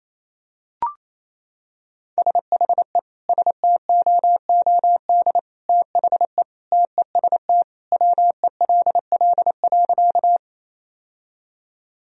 Morse code, heard as a keyed tone, about 28 wpm, 700 Hz. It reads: SHE STOOD THE TEST WELL.
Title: Sentences from Top 1000 Words - Review Part 3 - 28wpm